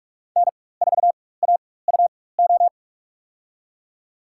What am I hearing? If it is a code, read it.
Morse code, heard as a keyed tone, about 46 wpm, spaced as for 21 wpm, 700 Hz. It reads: N4AUO
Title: USA Call Signs - 4x Character Spacing - 45wpm